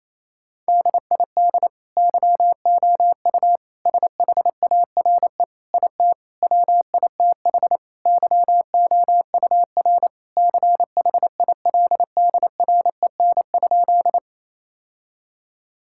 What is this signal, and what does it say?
Morse code, heard as a keyed tone, about 28 wpm, 700 Hz. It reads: DID YOU SHARE IT WITH YOUR CHILDREN?
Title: Sentences from Top 1000 Words - Review Part 6 - 28wpm